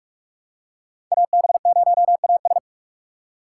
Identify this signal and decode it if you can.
Morse code, heard as a keyed tone, about 44 wpm, 700 Hz. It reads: AB0AS
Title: USA Call Signs - 5 Characters - 6x Character Spacing - 45wpm